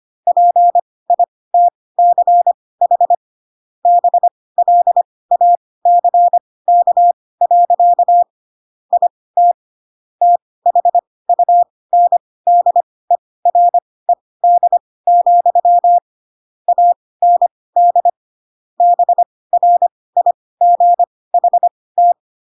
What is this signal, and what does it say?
Morse code, heard as a keyed tone, about 25 wpm, spaced as for 18 wpm, 700 Hz. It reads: PITCH BLACK. IT THUNDERED, AND BRIGHT
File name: Pinocchio / Pinocchio ch6 12 25 ik1hge